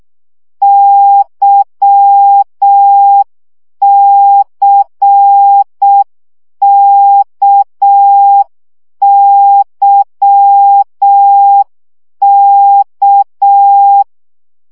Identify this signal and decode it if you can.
Morse code, heard as a keyed tone, about 6 wpm, 800 Hz. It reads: YCKYK
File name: CW gyakorlas k c y 5 baud